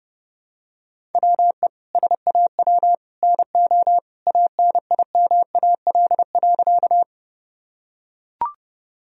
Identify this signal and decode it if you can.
Morse code, heard as a keyed tone, about 30 wpm, 700 Hz. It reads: WE SAW NO ANIMAL.
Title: Sentences from Top 1000 Words - Review Part 1 - 30wpm